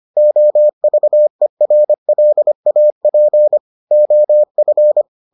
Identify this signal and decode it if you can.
Morse code, heard as a keyed tone, about 25 wpm, 600 Hz. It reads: OVERLAP OF